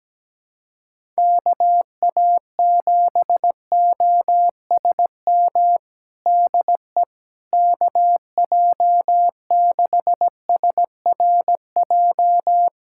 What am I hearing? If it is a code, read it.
Morse code, heard as a keyed tone, about 17 wpm, 700 Hz. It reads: KA7OSM DE KJ6SRJ
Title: USA Callsign DE Callsign 17wpm